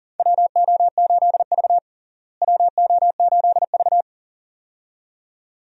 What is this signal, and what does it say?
Morse code, heard as a keyed tone, about 40 wpm, 700 Hz. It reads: WO8V WO8V
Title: USA Call Signs - 4 Characters - Repeated 2x - 40wpm